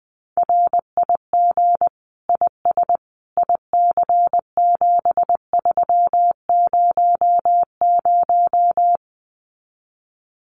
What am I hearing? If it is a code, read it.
Morse code, heard as a keyed tone, about 20 wpm, 700 Hz. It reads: RIG IS IC7300
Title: Common QSO Phrases Encore 20wpm